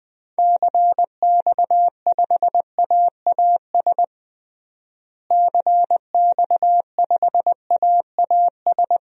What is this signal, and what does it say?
Morse code, heard as a keyed tone, about 20 wpm, 700 Hz. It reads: CX5AAS CX5AAS
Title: DXCC-100 Call Signs - 6 Characters - Repeated 3x - 20wpm